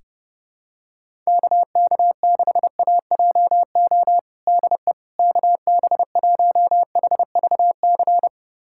Morse code, heard as a keyed tone, about 30 wpm, 700 Hz. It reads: KK6AJO DE KB1HVC
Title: USA Callsign DE Callsign 30wpm